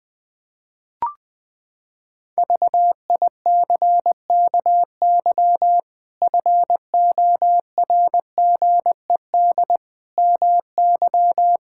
Morse code, heard as a keyed tone, about 20 wpm, 700 Hz. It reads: VICKY FORGED MY
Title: Pangram Challenge v2 - 20wpm